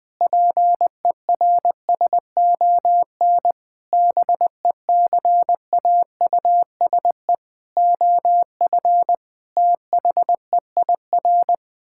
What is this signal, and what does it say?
Morse code, heard as a keyed tone, about 21 wpm, 700 Hz. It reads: PERSON BECAUSE OF THEIR
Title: Sentences with 9-Word Phrases - 20wpm